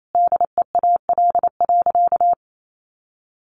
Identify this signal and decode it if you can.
Morse code, heard as a keyed tone, about 28 wpm, 700 Hz. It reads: DEAL.
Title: Sentences from Top 1200 Words 28wpm